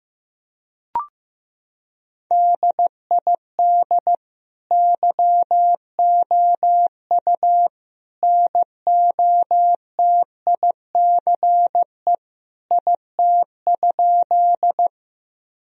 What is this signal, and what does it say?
Morse code, heard as a keyed tone, about 16 wpm, 700 Hz. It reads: DID YOU NOTICE IT?
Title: Sentences from Top 1000 Words - Review Part 2 - 15wpm